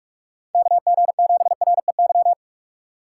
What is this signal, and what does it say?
Morse code, heard as a keyed tone, about 45 wpm, 700 Hz. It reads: KG7REY